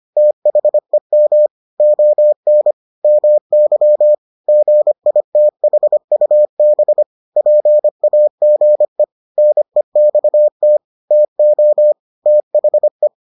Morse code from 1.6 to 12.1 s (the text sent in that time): ON MY GITHUB PAGE NEXT TO